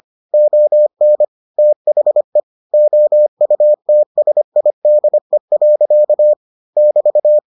Morse code, heard as a keyed tone, about 25 wpm, 600 Hz. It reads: ON THE OUTSIDE. <BT>